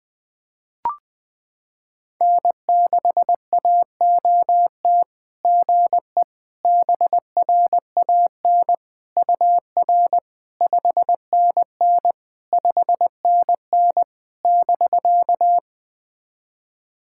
Morse code, heard as a keyed tone, about 21 wpm, 690 Hz. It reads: N6AOT GE BRAN UR 5NN 5NN <BK>
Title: SOTA QSO Practice 20wpm